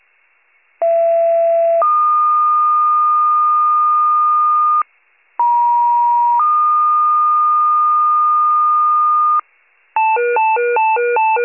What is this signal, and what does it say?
Two-tone page, heard as a one-second tone, then a three-second tone, about 670 then 1220 Hz, about 950 then 1220 Hz. It is heard from 0.8 to 9.4 s.